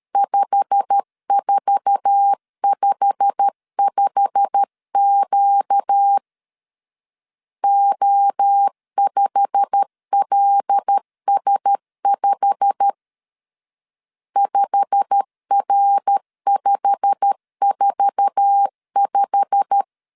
Morse code, heard as a keyed tone, about 13 wpm, 800 Hz. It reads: 5455Q O5LS5 5R545
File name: LEZIONI / 8 YJ45 / 8 YJ45 12wpm 2151